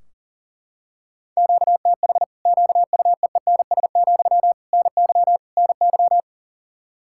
Morse code, heard as a keyed tone, about 40 wpm, 700 Hz. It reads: QTH QUEENS, NY NY
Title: Practice QSO - Exchange 1 - Beginning - 40wpm